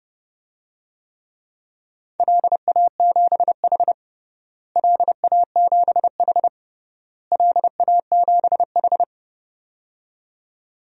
Morse code, heard as a keyed tone, about 30 wpm, 700 Hz. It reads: LA7H LA7H LA7H
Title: DXCC-100 Call Signs - 4 Characters - Repeated 3x - 30wpm